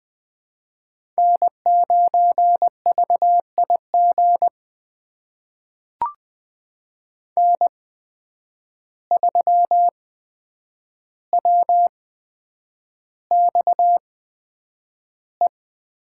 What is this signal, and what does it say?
Morse code, heard as a keyed tone, about 20 wpm, spaced as for 9 wpm, 700 Hz. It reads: N9VIG N3WXE
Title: USA Call Signs - 5 Characters - 8x Character Spacing - 20wpm